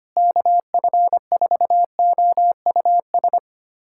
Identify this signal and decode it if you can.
Morse code, heard as a keyed tone, about 25 wpm, 700 Hz. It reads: KF4OUS
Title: USA Callsign DE Callsign - 6 Characters - 25wpm